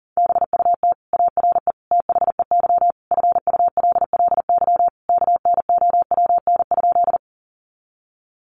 Morse code, heard as a keyed tone, about 40 wpm, 700 Hz. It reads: BUT ARE THEY FULLY KNOWN?
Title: Sentences from Top 1300 Words 40wpm